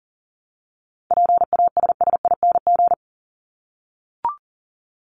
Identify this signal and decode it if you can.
Morse code, heard as a keyed tone, about 40 wpm, 700 Hz. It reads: PASSING
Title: ING Suffix - Top 100 Words with Base Words 40wpm